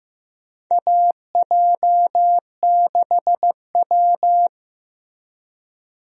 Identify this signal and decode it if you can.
Morse code, heard as a keyed tone, about 15 wpm, 700 Hz. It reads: AJ6W
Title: USA Call Signs - 4 Characters - Repeated 2x - 15wpm